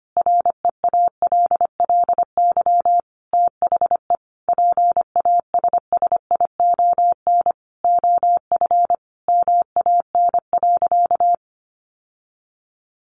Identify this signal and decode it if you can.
Morse code, heard as a keyed tone, about 25 wpm, 700 Hz. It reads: REALLY THE PASSION OF MAN.
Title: Sentences from Top 1400 Words 25wpm